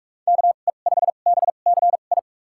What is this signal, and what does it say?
Morse code, heard as a keyed tone, about 43 wpm, spaced as for 32 wpm, 700 Hz. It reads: KE5BCI
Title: USA Call Signs - 6 Characters - 2x Character Spacing - 45wpm